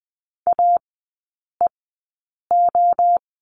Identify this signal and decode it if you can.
Morse code, heard as a keyed tone, about 20 wpm, spaced as for 8 wpm, 700 Hz. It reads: AEO
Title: 067 Lesson 12e - Letter d - Three Characters 20wpm